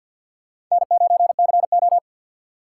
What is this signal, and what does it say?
Morse code, heard as a keyed tone, about 53 wpm, 700 Hz. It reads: N9CO
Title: USA Call Signs - 4 Characters - 6x Character Spacing - 50wpm